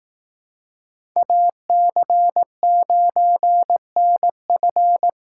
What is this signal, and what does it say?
Morse code, heard as a keyed tone, about 18 wpm, 700 Hz. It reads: AC9NF
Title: Speed Racing - USA Call Signs 15wpm